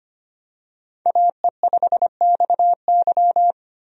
Morse code, heard as a keyed tone, about 25 wpm, 700 Hz. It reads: AE5XY
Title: USA Call Signs - 5 Characters - 25wpm